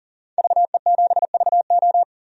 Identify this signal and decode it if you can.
Morse code, heard as a keyed tone, about 40 wpm, 700 Hz. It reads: VE7VO
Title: DXCC-100 Call Signs - 5 Characters - 40wpm